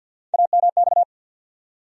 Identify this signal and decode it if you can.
Morse code, heard as a keyed tone, about 50 wpm, 700 Hz. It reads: AMX
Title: USA Call Sign Postfixes - 3 Letters 50wpm